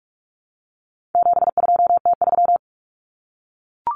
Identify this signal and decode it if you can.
Morse code, heard as a keyed tone, about 44 wpm, 700 Hz. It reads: 72T3
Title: Common Cut Numbers - 4-Digit 45wpm